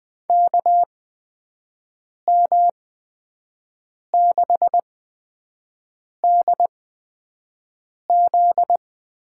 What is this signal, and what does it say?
Morse code, heard as a keyed tone, about 20 wpm, spaced as for 5 wpm, 700 Hz. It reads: KM6DZ
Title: USA Call Signs - 6 Characters - 8x Character Spacing - 20wpm